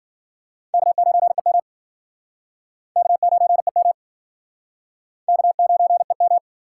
Morse code, heard as a keyed tone, about 55 wpm, 700 Hz. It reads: K9EM K9EM K9EM